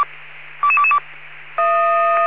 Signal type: five-tone selective call